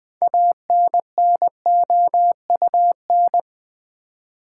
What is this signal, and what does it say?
Morse code, heard as a keyed tone, about 20 wpm, 700 Hz. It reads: ANNOUN
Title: Build A Word - 2 Letters At A Time - 8-Letter Words - 20wpm